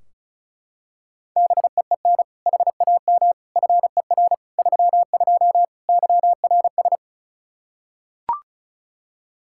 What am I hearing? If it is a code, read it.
Morse code, heard as a keyed tone, about 36 wpm, 700 Hz. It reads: BEEN HAM FER 32 YRS
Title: K1USN SST QSO Practice 35wpm